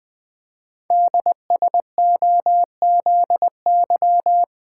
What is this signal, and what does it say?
Morse code, heard as a keyed tone, about 20 wpm, 700 Hz. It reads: DSOZY
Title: Code Groups - 5 Letters - 20wpm